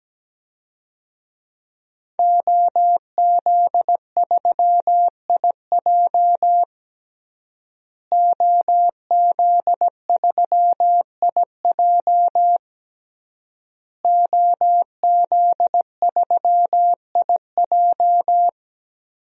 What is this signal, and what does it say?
Morse code, heard as a keyed tone, about 17 wpm, 700 Hz. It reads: OZ3IJ OZ3IJ OZ3IJ
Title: DXCC-100 Call Signs - 5 Characters - Repeated 3x - 17wpm